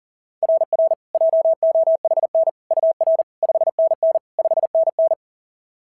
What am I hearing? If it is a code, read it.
Morse code, heard as a keyed tone, about 37 wpm, 690 Hz. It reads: RR JOHN UR 5NN 5NN
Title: SOTA QSO Practice 40wpm